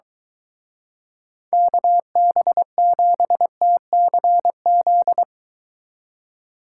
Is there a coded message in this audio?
Yes: Morse code from 1.5 to 5.2 s, heard as a keyed tone, about 23 wpm, 700 Hz. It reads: KB7TCZ